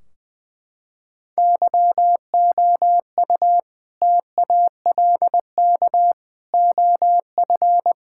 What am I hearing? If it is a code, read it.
Morse code, heard as a keyed tone, about 21 wpm, 700 Hz. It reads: YOU TALK OF
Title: Sentences from Top 1600 Words 20wpm